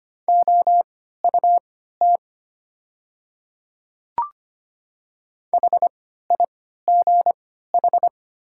Morse code, heard as a keyed tone, about 25 wpm, spaced as for 14 wpm, 700 Hz. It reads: OUT HIGH
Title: 3x Character Spacing - Top 300 Words - 25wpm